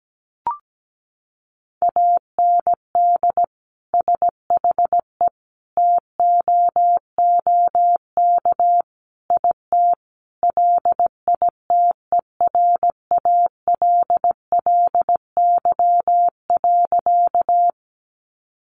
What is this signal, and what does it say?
Morse code, heard as a keyed tone, about 17 wpm, 700 Hz. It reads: AND SHE TOOK IT LITERALLY.
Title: Sentences from Top 1400 Words 17wpm